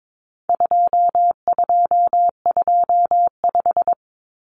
Morse code, heard as a keyed tone, about 22 wpm, 700 Hz. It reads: 2225
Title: Common Cut Numbers - 4-Digit 22wpm